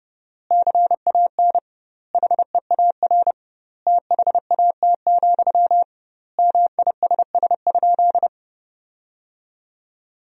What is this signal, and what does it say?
Morse code, heard as a keyed tone, about 30 wpm, 700 Hz. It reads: CAN HEAR THAT, MISS?
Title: 2x Word-Spacing - Sentences from Top 500 Words - 30wpm